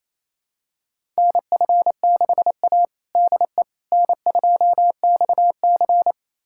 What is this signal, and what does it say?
Morse code, heard as a keyed tone, about 28 wpm, 700 Hz. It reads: NF6A DE N2XC
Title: USA Callsign DE Callsign - 4 Characters - 28wpm